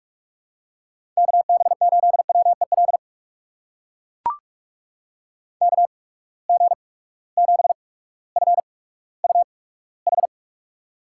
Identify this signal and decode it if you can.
Morse code, heard as a keyed tone, about 44 wpm, spaced as for 19 wpm, 700 Hz. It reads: KB8WEL KG7FUH
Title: USA Call Signs - 6 Characters - 8x Character Spacing - 45wpm